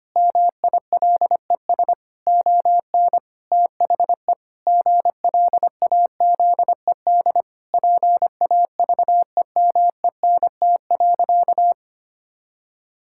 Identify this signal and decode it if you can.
Morse code, heard as a keyed tone, about 25 wpm, 700 Hz. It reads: MILES ON THE GLAZED PAVEMENT.
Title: Pangram Challenge v2 - 25wpm